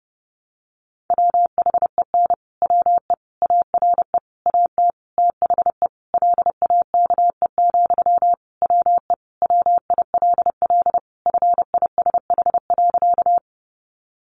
Morse code, heard as a keyed tone, about 30 wpm, 700 Hz. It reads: WHEN WE ARE AT THE LAKE, WE WILL FISH.
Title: Sentences from Top 1100 Words 30wpm